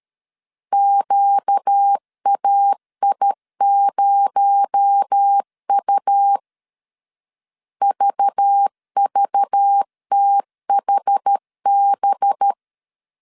Morse code, heard as a keyed tone, about 13 wpm, 800 Hz. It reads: QAI0U VVTHB
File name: CW / DET / 1 6 1500C S A 12wpm 3440